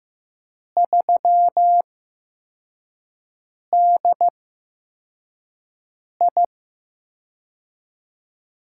Morse code, heard as a keyed tone, about 15 wpm, spaced as for 4 wpm, 700 Hz. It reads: 3DI
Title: USA Call Signs - 8x Character Spacing - 15wpm